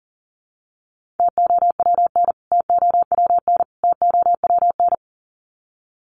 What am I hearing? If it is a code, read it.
Morse code, heard as a keyed tone, about 40 wpm, 700 Hz. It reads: TOWN TOWN TOWN
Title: Top 500 Words Repeated 40wpm